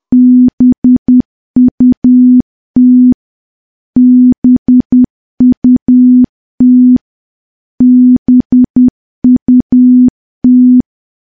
Morse code, heard as a keyed tone, about 10 wpm, 250 Hz. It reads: BUT BUT BUT
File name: morse / morse-training-10WPM